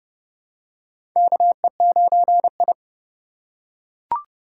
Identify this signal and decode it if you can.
Morse code, heard as a keyed tone, about 30 wpm, 700 Hz. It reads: KE9I
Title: K1USN SST QSO Practice - 30wpm